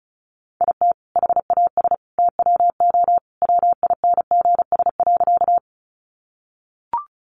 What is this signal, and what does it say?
Morse code, heard as a keyed tone, about 34 wpm, 700 Hz. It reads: IT HAS TWO WINGS.
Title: Additional Easy Phrases 35wpm